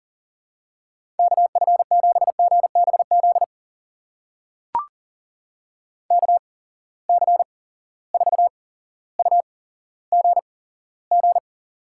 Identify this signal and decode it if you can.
Morse code, heard as a keyed tone, about 40 wpm, spaced as for 17 wpm, 700 Hz. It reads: KF7GBZ KC4UGG
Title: USA Call Signs - 6 Characters - 8x Character Spacing - 40wpm